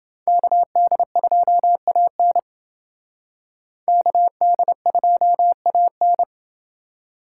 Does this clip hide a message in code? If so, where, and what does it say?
Yes: Morse code from 0.3 to 6.2 s, heard as a keyed tone, about 28 wpm, 700 Hz. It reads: KD2AN KD2AN